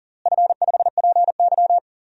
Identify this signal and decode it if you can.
Morse code, heard as a keyed tone, about 40 wpm, 700 Hz. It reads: F5PY
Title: DXCC-100 Call Signs - 4 Characters - Repeated 2x - 40wpm